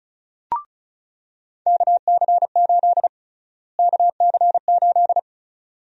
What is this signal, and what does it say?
Morse code, heard as a keyed tone, about 36 wpm, 700 Hz. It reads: KC8 KC8
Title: USA Call Sign Prefixes - Repeated 2x - 35wpm